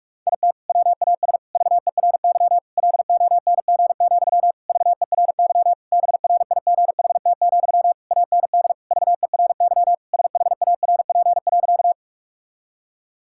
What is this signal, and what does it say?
Morse code, heard as a keyed tone, about 46 wpm, 700 Hz. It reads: IT WAS VERY LONG, VERY BRIGHT, AND VERY SHARP.
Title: Sentences from Top 1600 Words 45wpm